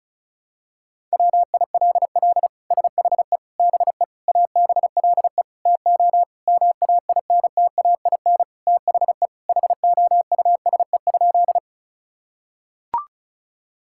Morse code, heard as a keyed tone, about 36 wpm, 700 Hz. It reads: WILL SHE BE ABLE TO MAINTAIN THE HOUSE?
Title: Sentences from Top 1000 Words - Review Part 5 - 35wpm